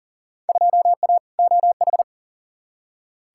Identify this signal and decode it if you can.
Morse code, heard as a keyed tone, about 40 wpm, 700 Hz. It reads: 2A OH
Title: Field Day Exchanges - 40wpm